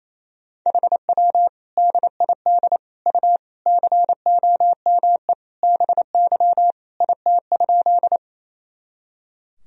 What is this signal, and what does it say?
Morse code, heard as a keyed tone, about 28 wpm, 700 Hz. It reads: HW DID U COME BY IT?
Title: Sentences from Top 100 Words and Abbreviations 28wpm